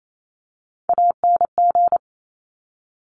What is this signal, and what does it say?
Morse code, heard as a keyed tone, about 28 wpm, 700 Hz. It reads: ANG
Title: Top Three Letter Combinations in Words 28wpm